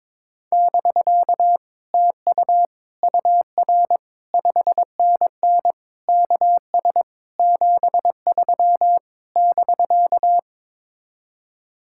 Morse code, heard as a keyed tone, about 22 wpm, 700 Hz. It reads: <BK> TU UR 5NN KS 73 <BK>
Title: Speed-Racing Common QSO Phrases Encore - 22wpm